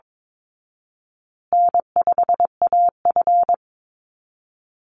Morse code, heard as a keyed tone, about 22 wpm, 700 Hz. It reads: N5AF
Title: Easy Call Signs Repeated 3x - 22wpm